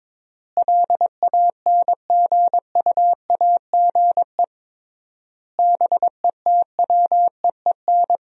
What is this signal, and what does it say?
Morse code, heard as a keyed tone, about 22 wpm, 700 Hz. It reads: LANGUAGE BETWEEN
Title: 3x Word-Spacing - Sentences from Top 500 Words - 22wpm